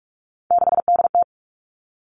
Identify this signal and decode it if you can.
Morse code, heard as a keyed tone, about 44 wpm, 700 Hz. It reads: 6DT
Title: All Cut Numbers - 3-Digit 45wpm